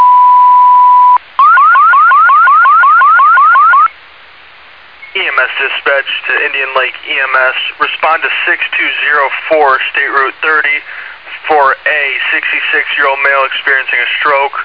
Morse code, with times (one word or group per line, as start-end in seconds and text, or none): none